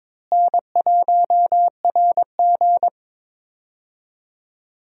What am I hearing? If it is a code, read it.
Morse code, heard as a keyed tone, about 22 wpm, 700 Hz. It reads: N1RG
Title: USA Call Signs - 4 Characters - Repeated 2x - 22wpm